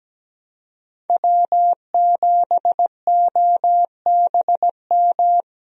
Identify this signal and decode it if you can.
Morse code, heard as a keyed tone, about 17 wpm, 700 Hz. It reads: W7OBM
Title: USA Call Signs - 5 Characters - Repeated 2x - 17wpm